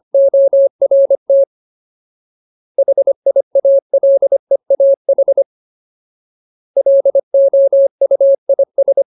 Morse code, heard as a keyed tone, about 25 wpm, 550 Hz. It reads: ORT HIALEAH LOUIS
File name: Ultimate Word List / 25WPM Ultimate